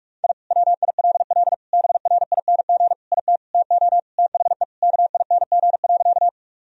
Morse code, heard as a keyed tone, about 41 wpm, 700 Hz. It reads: I WILL BRING IT TO THE KING.
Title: Sentences from Top 1000 Words - Review Part 3 - 45wpm